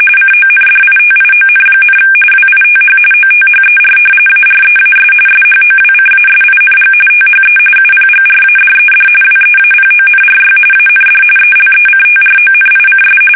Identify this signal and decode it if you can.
RTTY, about 45.45 baud. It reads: CONDITION, WITH OTHER AREAS SHOWCASING THE UPGRADED WEAPONRY AND ELECTRONICS
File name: fsk850 / battleshipnj